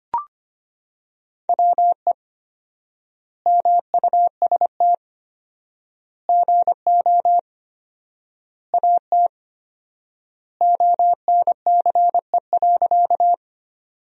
Morse code, heard as a keyed tone, about 25 wpm, 700 Hz. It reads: WE MUST GO AT ONCE.
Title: 4x Word-Spacing - Sentences from Top 300 Words - 25wpm